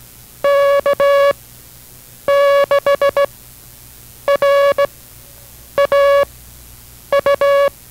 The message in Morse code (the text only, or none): K6RAU